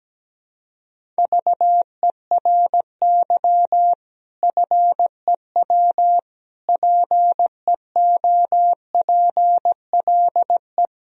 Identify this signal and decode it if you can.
Morse code, heard as a keyed tone, about 17 wpm, 700 Hz. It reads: VERY FEW PEOPLE
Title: Sentences from Top 1000 Words - Review Part 1 - 17wpm